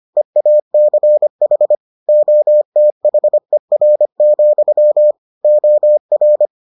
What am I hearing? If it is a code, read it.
Morse code, heard as a keyed tone, about 25 wpm, 600 Hz. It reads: EACH OTHER, OR